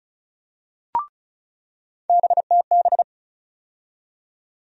Morse code, heard as a keyed tone, about 36 wpm, 700 Hz. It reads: BTB
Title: USA Call Sign Postfixes - 3 Letters 35wpm